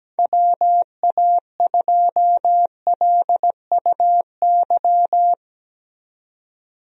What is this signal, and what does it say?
Morse code, heard as a keyed tone, about 17 wpm, 700 Hz. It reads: WA2LUY
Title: USA Call Signs - 6 Characters - 17wpm